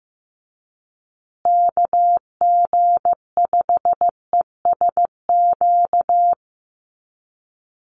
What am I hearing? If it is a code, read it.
Morse code, heard as a keyed tone, about 15 wpm, 700 Hz. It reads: KG5ESQ
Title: USA Call Signs Repeated 3x 15wpm